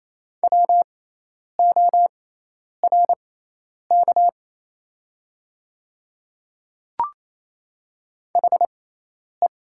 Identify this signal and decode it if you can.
Morse code, heard as a keyed tone, about 28 wpm, spaced as for 10 wpm, 700 Hz. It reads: WORK HE
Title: 6x Character Spacing - Top 200 Words - 28wpm